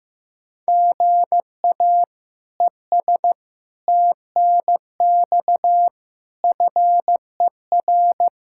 Morse code, heard as a keyed tone, about 15 wpm, 700 Hz. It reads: GA ES TNX FER
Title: K1USN SST QSO Practice 15wpm